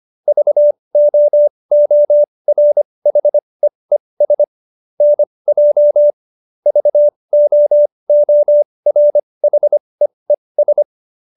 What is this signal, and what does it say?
Morse code, heard as a keyed tone, about 25 wpm, spaced as for 20 wpm, 590 Hz. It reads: VOORHEES NJ VOORHEES